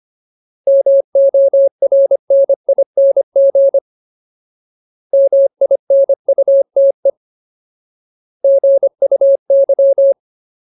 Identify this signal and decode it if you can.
Morse code, heard as a keyed tone, about 25 wpm, 550 Hz. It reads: MORNING MINUTE GUY
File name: Ultimate Word List / 25WPM Ultimate